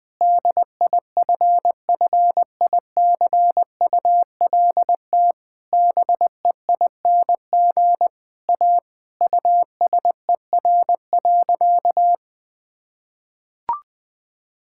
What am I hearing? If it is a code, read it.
Morse code, heard as a keyed tone, about 20 wpm, 700 Hz. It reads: DIFFICULT BEING A USER.
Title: Sentences from Top 1000 Words - Review Part 5 - 20wpm